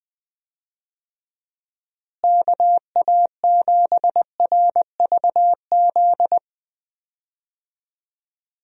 Morse code, heard as a keyed tone, about 20 wpm, 700 Hz. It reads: KA7RVZ
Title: USA Call Signs - 6 Characters - Repeated 2x - 20wpm